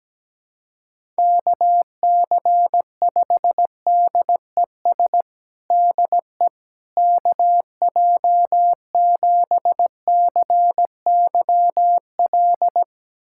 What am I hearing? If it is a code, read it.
Morse code, heard as a keyed tone, about 17 wpm, 700 Hz. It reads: KC5DES DE KJ7CYL
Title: USA Callsign DE Callsign - 6 Characters - 17wpm